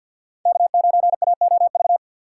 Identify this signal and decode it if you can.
Morse code, heard as a keyed tone, about 49 wpm, 700 Hz. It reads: K9AOV
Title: USA Call Signs - 5 Characters - 3x Character Spacing - 50wpm